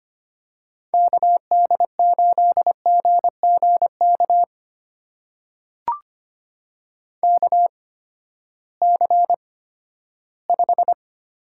Morse code, heard as a keyed tone, about 25 wpm, spaced as for 14 wpm, 700 Hz. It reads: KD8GGK KC5
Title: USA Call Signs - 6 Characters - 8x Character Spacing - 25wpm